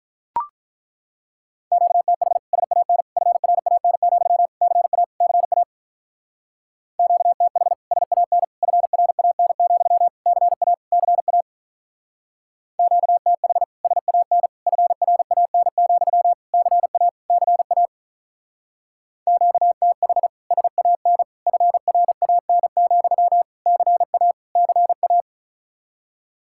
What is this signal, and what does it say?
Morse code, heard as a keyed tone, about 43 wpm, 700 Hz. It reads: QTH SAN FRAN, CA CA QTH SAN FRAN, CA CA QTH SAN FRAN, CA CA QTH SAN FRAN, CA CA
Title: Speed-Racing Common QSO Phrases - 35wpm